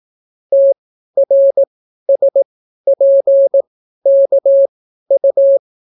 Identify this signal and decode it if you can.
Morse code, heard as a keyed tone, about 18 wpm, spaced as for 12 wpm, 550 Hz. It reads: TRSPKU